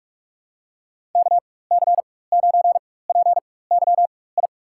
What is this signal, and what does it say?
Morse code, heard as a keyed tone, about 47 wpm, spaced as for 21 wpm, 700 Hz. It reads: KC9PYI
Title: USA Call Signs - 6 Characters - 4x Character Spacing - 45wpm